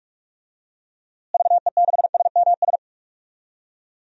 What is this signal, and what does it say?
Morse code, heard as a keyed tone, about 44 wpm, 700 Hz. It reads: VE6SMS
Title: DXCC-100 Call Signs - 6 Characters - Repeated 2x - 45wpm